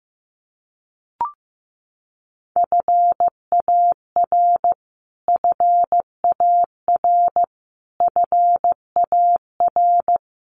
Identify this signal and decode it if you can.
Morse code, heard as a keyed tone, about 15 wpm, 700 Hz. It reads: FAR FAR FAR
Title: Top 300 Words Repeated 15wpm